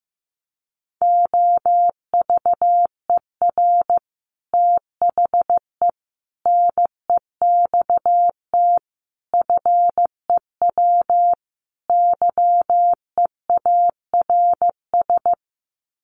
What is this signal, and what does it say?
Morse code, heard as a keyed tone, about 14 wpm, 700 Hz. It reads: OVER THE NEXT FEW YEARS
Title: Sets of 5 Words Encore 15wpm